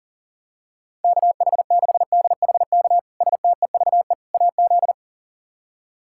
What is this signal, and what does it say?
Morse code, heard as a keyed tone, about 40 wpm, 700 Hz. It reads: KH6DHK STEVE AZ
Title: K1USN SST QSO Practice - 40wpm